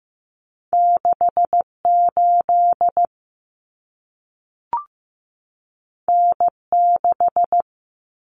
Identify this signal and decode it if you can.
Morse code, heard as a keyed tone, about 15 wpm, 700 Hz. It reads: 68 N6
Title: Common Cut Numbers - 2-Digit 15wpm